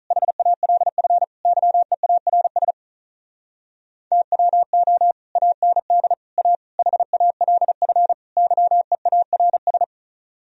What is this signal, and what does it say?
Morse code, heard as a keyed tone, about 43 wpm, 700 Hz. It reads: HALF YEARS TWO AND A HALF YEARS
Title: Speed-Racing - Sets of 5 Words 35wpm